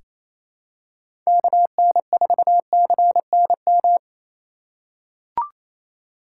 Morse code, heard as a keyed tone, about 27 wpm, 700 Hz. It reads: KN4CNM
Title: USA Call Signs - 6 Characters - 3x Character Spacing - 28wpm